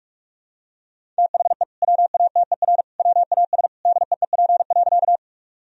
Morse code, heard as a keyed tone, about 41 wpm, 700 Hz. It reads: THE WATER WAS DEEP.
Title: Sentences from Top 1000 Words - Review Part 3 - 45wpm